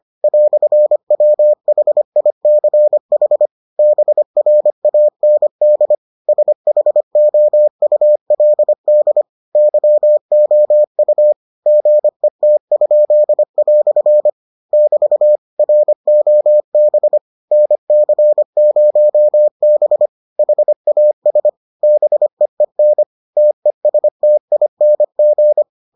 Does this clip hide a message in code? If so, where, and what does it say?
Yes: Morse code from 0.2 to 25.6 s, heard as a keyed tone, about 25 wpm, 600 Hz. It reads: "WHICH BRAND SHOULD YOU GET?" <BT> ROB NC0B HAS BEEN TESTING